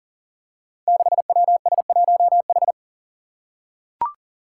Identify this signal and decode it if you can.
Morse code, heard as a keyed tone, about 40 wpm, 700 Hz. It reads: 6WS1H
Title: Five Challenging Letter-Number Combinations 40wpm